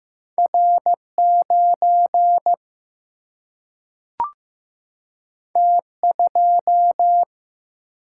Morse code, heard as a keyed tone, about 15 wpm, 700 Hz. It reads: R9 T2
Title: Single-Letter Single-Number 15wpm